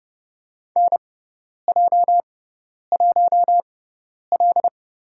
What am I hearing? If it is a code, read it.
Morse code, heard as a keyed tone, about 32 wpm, spaced as for 10 wpm, 700 Hz. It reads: NJ1L
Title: USA Call Signs - 4 Characters - 6x Character Spacing - 30wpm